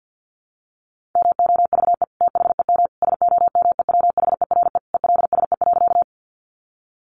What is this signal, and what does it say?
Morse code, heard as a keyed tone, about 50 wpm, 700 Hz. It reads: MOVE THEM SOMEWHERE ELSE.
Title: Sentences from Top 1400 Words 50wpm